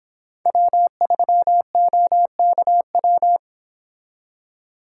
Morse code, heard as a keyed tone, about 26 wpm, 700 Hz. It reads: W3OKW
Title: Speed Racing - USA Call Signs 22wpm